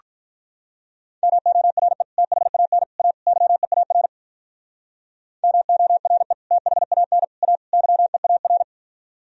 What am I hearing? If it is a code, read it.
Morse code, heard as a keyed tone, about 44 wpm, 700 Hz. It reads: MORE THAN A YEAR MORE THAN A YEAR